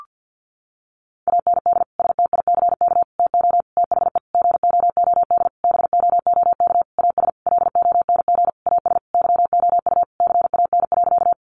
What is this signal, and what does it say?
Morse code, heard as a keyed tone, about 50 wpm, 700 Hz. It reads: AND STICK TO THE GOOD BOOK AS LONG AS YOU CAN.
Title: Sentences from Top 1200 Words 50wpm